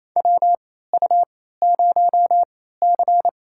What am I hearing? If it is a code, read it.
Morse code, heard as a keyed tone, about 28 wpm, spaced as for 16 wpm, 700 Hz. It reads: WU0C
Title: USA Call Signs - 4 Characters - 3x Character Spacing - 28wpm